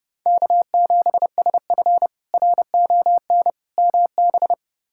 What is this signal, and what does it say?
Morse code, heard as a keyed tone, about 32 wpm, 700 Hz. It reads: K7SF RON MB
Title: K1USN SST QSO Practice - 30wpm